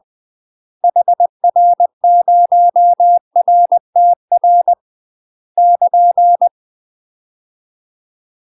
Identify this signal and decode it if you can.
Morse code, heard as a keyed tone, about 20 wpm, 700 Hz. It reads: HR0RTR <KN>
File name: UK MASTER QSOs TRAINING / 20WPM / UK MASTER QSOs 0071